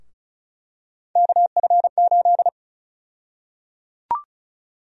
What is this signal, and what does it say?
Morse code, heard as a keyed tone, about 36 wpm, 700 Hz. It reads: KF8
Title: USA Call Sign Prefixes - 35wpm